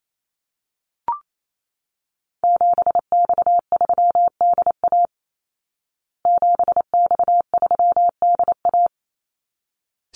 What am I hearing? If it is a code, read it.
Morse code, heard as a keyed tone, about 28 wpm, 700 Hz. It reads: 7X3DA 7X3DA
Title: Call Signs Repeated 3x - 28wpm